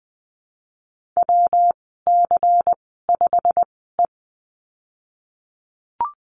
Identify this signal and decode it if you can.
Morse code, heard as a keyed tone, about 20 wpm, spaced as for 14 wpm, 700 Hz. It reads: WC5E
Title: 179 Lesson 29g - Slash - Call Signs 20wpm